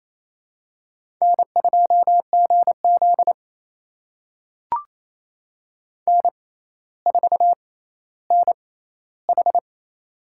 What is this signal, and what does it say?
Morse code, heard as a keyed tone, about 28 wpm, spaced as for 14 wpm, 700 Hz. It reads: N2GZ N4NH
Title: USA Call Signs - 4 Characters - 6x Character Spacing - 28wpm